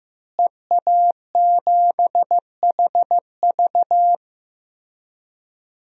Morse code, heard as a keyed tone, about 15 wpm, 700 Hz. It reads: EA7HV
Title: DXCC-100 Call Signs - 5 Characters - Repeated 2x - 15wpm